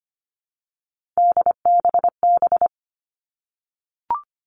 Morse code, heard as a keyed tone, about 25 wpm, 700 Hz. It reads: DBB
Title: All Cut Numbers - 3-Digit 25wpm